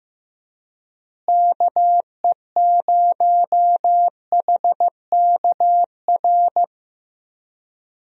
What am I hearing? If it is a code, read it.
Morse code, heard as a keyed tone, about 16 wpm, 700 Hz. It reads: KE0HKR
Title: USA Call Signs - 6 Characters - Repeated 3x - 15wpm